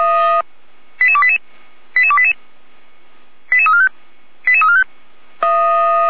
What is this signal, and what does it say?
Five-tone selective call (ZVEI), heard as quick runs of five tones, tones of about 70 ms. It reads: 88288 88288 88345 88345